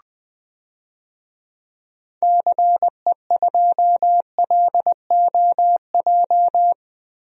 Morse code, heard as a keyed tone, about 20 wpm, 700 Hz. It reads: CE2LOJ
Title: DXCC-100 Call Signs - 6 Characters - Repeated 3x - 20wpm